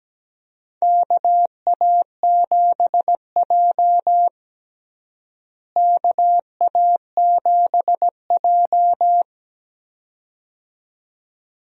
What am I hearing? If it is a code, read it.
Morse code, heard as a keyed tone, about 17 wpm, 700 Hz. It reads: KA7J KA7J
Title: USA Call Signs - 4 Characters - Repeated 2x - 17wpm